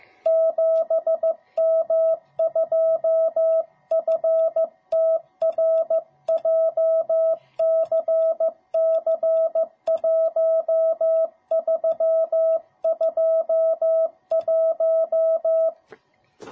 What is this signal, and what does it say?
Morse code, heard as a keyed tone, about 15 wpm, 660 Hz. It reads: 7M2FTR JCC1321